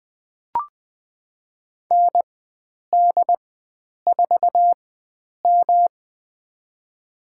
Morse code, heard as a keyed tone, about 21 wpm, spaced as for 9 wpm, 700 Hz. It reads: ND4M
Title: USA Call Signs - 4 Characters - 4x Character Spacing - 20wpm